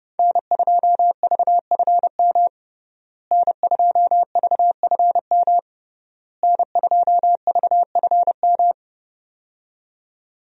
Morse code, heard as a keyed tone, about 30 wpm, 700 Hz. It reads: N2VFM N2VFM N2VFM
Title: USA Call Signs - 5 Characters - Repeated 3x - 30wpm